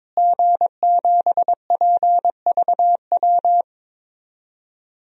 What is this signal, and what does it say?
Morse code, heard as a keyed tone, about 22 wpm, 700 Hz. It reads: G7PVW
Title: DXCC-100 Call Signs - 5 Characters - Repeated 3x - 22wpm